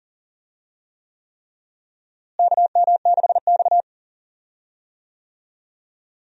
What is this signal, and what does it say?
Morse code, heard as a keyed tone, about 40 wpm, 700 Hz. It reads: KM6X
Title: USA Call Signs - 4 Characters - Repeated 3x - 40wpm